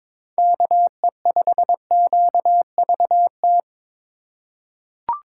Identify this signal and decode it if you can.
Morse code, heard as a keyed tone, about 22 wpm, 700 Hz. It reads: KE5QVT
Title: USA Call Signs - 6 Characters - 4x Character Spacing - 22wpm